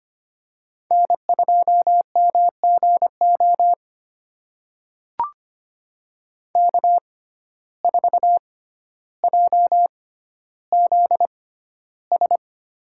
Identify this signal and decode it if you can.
Morse code, heard as a keyed tone, about 25 wpm, spaced as for 13 wpm, 700 Hz. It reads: N2MGO K4JZS
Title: USA Call Signs - 5 Characters - 6x Character Spacing - 25wpm